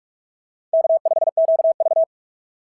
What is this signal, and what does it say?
Morse code, heard as a keyed tone, about 46 wpm, 680 Hz. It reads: K5QV
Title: SOTA QSO Practice 45wpm